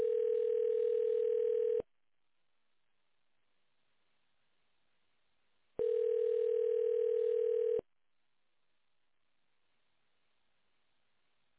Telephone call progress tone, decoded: ringback tone